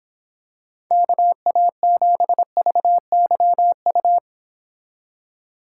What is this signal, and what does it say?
Morse code, heard as a keyed tone, about 26 wpm, 700 Hz. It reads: KA7VYU